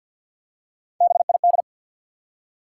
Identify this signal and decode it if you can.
Morse code, heard as a keyed tone, about 48 wpm, 700 Hz. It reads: BID
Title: Three Letter Words - 50wpm